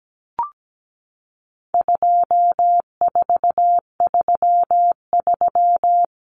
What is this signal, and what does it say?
Morse code, heard as a keyed tone, about 17 wpm, 700 Hz. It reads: 2433
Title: Common Cut Numbers - 4-Digit 17wpm